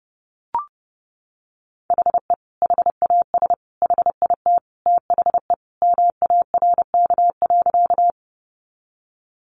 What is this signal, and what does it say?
Morse code, heard as a keyed tone, about 30 wpm, 700 Hz. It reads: HE HAS HIT THE MARK.
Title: Sentences from Top 1200 Words 30wpm